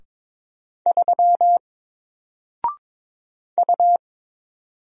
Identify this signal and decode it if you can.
Morse code, heard as a keyed tone, about 22 wpm, 700 Hz. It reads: 3 U
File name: Single-Letter-Number-22wpm